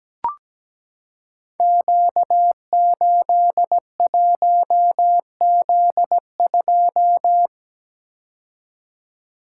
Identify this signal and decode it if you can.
Morse code, heard as a keyed tone, about 17 wpm, 700 Hz. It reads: Q81Z2
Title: Code Groups - 5 Letter:Numbers - 17wpm